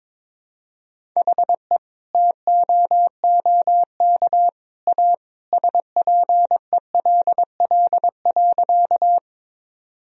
Morse code, heard as a keyed tone, about 22 wpm, 700 Hz. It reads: HE TOOK A SPELL.
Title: Sentences from Top 1000 Words - Review Part 1 - 22wpm